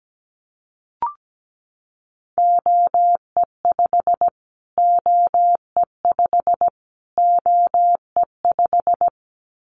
Morse code, heard as a keyed tone, about 17 wpm, 700 Hz. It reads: OE5 OE5 OE5
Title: Callsign Prefixes for Top 25 DXCC - Repeated 3x - 17wpm